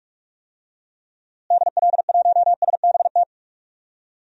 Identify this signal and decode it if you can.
Morse code, heard as a keyed tone, about 46 wpm, 700 Hz. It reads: DL1SBT
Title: DXCC-100 Call Signs - 6 Characters - Repeated 3x - 45wpm